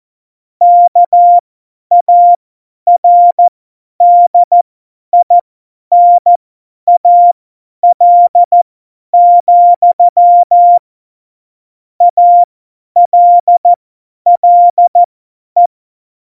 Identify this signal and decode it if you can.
Morse code, heard as a keyed tone, about 14 wpm, spaced as for 10 wpm, 700 Hz. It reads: KARDINAL, ALLE
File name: Morsekurs / Witze / Witz 27. Oktober 2020, 50 F70